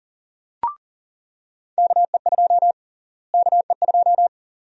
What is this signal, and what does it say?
Morse code, heard as a keyed tone, about 40 wpm, 700 Hz. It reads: KE2 KE2
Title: USA Call Sign Prefixes - Repeated 2x - 40wpm